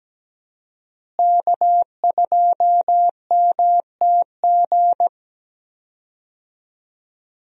Morse code, heard as a keyed tone, about 17 wpm, 700 Hz. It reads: K2MTG
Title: USA Call Signs - 5 Characters - Repeated 3x - 17wpm